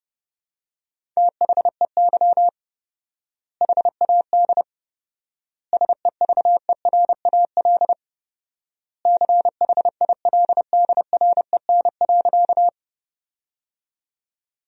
Morse code, heard as a keyed tone, about 32 wpm, 700 Hz. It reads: THEY HAD SEVERAL CHILDREN.
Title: 4x Word-Spacing - Sentences from Top 400 Words - 30wpm